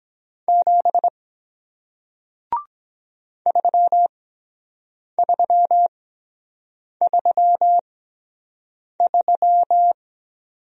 Morse code, heard as a keyed tone, about 23 wpm, spaced as for 7 wpm, 700 Hz. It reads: 7 3333